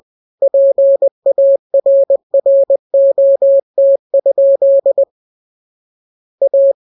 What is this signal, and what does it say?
Morse code, heard as a keyed tone, about 20 wpm, 550 Hz. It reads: PARROT? A